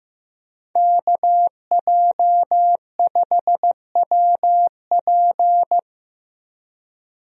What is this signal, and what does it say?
Morse code, heard as a keyed tone, about 15 wpm, 700 Hz. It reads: KJ5WP
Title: USA Call Signs - 5 Characters - 15wpm